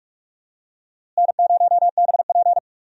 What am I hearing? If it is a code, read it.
Morse code, heard as a keyed tone, about 45 wpm, 700 Hz. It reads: N0BP
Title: USA Call Signs - 4 Characters - Repeated 3x - 45wpm